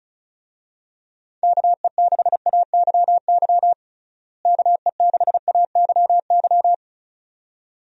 Morse code, heard as a keyed tone, about 36 wpm, 700 Hz. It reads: KE6AYY KE6AYY
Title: USA Call Signs - 6 Characters - Repeated 2x - 35wpm